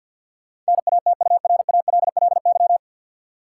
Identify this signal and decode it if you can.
Morse code, heard as a keyed tone, about 53 wpm, 700 Hz. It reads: NATURALLY
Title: Sentences from Top 1300 Words - New Word Drill Alt - 50wpm